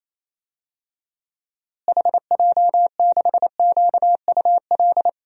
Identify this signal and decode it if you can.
Morse code, heard as a keyed tone, about 28 wpm, 700 Hz. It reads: HJ6QUL
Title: DXCC-100 Call Signs - 6 Characters - Repeated 3x - 28wpm